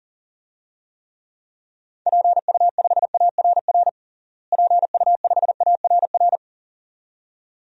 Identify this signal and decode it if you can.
Morse code, heard as a keyed tone, about 37 wpm, 700 Hz. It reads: PU5ARR PU5ARR
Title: DXCC-100 Call Signs - 6 Characters - Repeated 2x - 40wpm